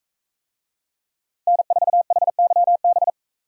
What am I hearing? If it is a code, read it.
Morse code, heard as a keyed tone, about 40 wpm, 700 Hz. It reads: N4HYB